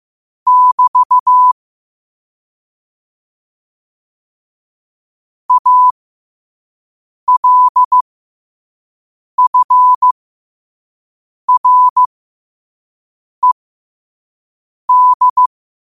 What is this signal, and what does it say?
Morse code, heard as a keyed tone, about 15 wpm, spaced as for 5 wpm, 1000 Hz. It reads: <BT> ALFRED
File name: qotdfiles / 5wpm